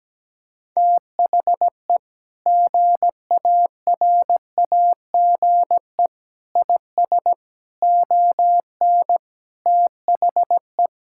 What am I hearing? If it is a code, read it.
Morse code, heard as a keyed tone, about 17 wpm, 700 Hz. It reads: THE GARAGE IS ON THE
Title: Sentences from Top 1600 Words 17wpm